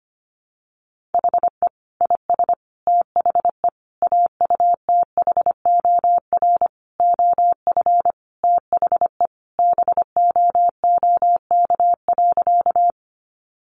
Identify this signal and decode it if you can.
Morse code, heard as a keyed tone, about 25 wpm, 700 Hz. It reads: HE IS THE AUTHOR OF THE BOOK.
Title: Sentences from Top 1200 Words 25wpm